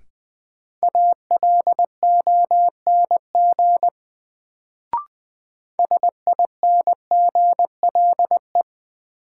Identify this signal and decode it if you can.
Morse code, heard as a keyed tone, about 20 wpm, 700 Hz. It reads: ALONG SINGLE
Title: Sentences from Top 700 Words - New Word Drill - 20wpm